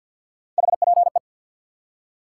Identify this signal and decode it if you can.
Morse code, heard as a keyed tone, about 48 wpm, 700 Hz. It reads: HPE
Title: Top 180 QSO Elements 50wpm